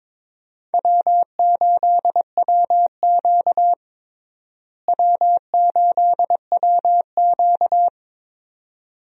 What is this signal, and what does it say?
Morse code, heard as a keyed tone, about 22 wpm, 700 Hz. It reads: W8WQ W8WQ
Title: USA Call Signs - 4 Characters - Repeated 2x - 22wpm